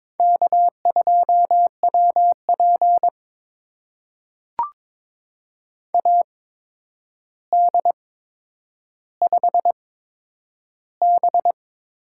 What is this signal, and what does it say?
Morse code, heard as a keyed tone, about 21 wpm, spaced as for 9 wpm, 700 Hz. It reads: K2WP AD5B
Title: USA Call Signs - 4 Characters - 8x Character Spacing - 22wpm